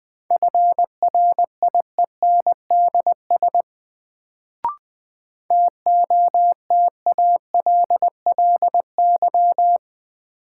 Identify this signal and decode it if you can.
Morse code, heard as a keyed tone, about 20 wpm, 700 Hz. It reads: FRIENDS TOTALLY
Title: Sentences from Top 1600 Words - New Word Drill Alt - 20wpm